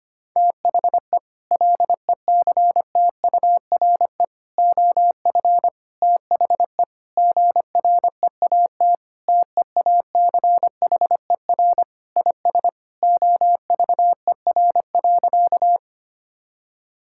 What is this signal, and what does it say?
Morse code, heard as a keyed tone, about 25 wpm, 700 Hz. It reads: THE LECTURE OF THE GREAT TEACHER IS OVER.
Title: Sentences from Top 1600 Words 25wpm